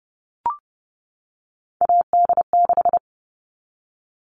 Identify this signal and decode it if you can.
Morse code, heard as a keyed tone, about 30 wpm, 700 Hz. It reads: AD6
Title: All Cut Numbers - 3-Digit 30wpm